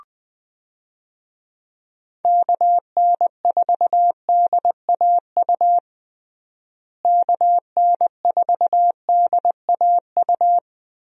Morse code, heard as a keyed tone, about 21 wpm, 700 Hz. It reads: KN4DAU KN4DAU
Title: USA Call Signs - 6 Characters - Repeated 2x - 20wpm